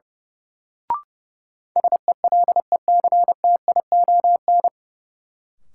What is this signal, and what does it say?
Morse code, heard as a keyed tone, about 32 wpm, 700 Hz. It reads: SELECTION
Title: Sentences from Top 1400 Words - New Word Drill - 30wpm